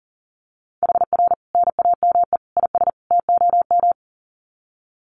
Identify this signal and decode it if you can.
Morse code, heard as a keyed tone, about 40 wpm, 700 Hz. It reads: HR NAME IS TOM
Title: Common QSO Phrases 40wpm